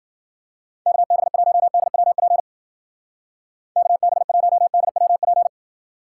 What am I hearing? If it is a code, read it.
Morse code, heard as a keyed tone, about 50 wpm, 700 Hz. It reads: KB1DWP KB1DWP
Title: Speed Racing - USA Call Signs 40wpm